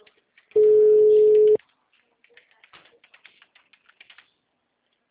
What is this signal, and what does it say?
Ringback tone (Europe, 425 Hz).